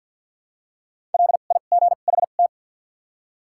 Morse code, heard as a keyed tone, about 54 wpm, 700 Hz. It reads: L I G H T
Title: 2x Character Spacing - Top 300 Words - 50wpm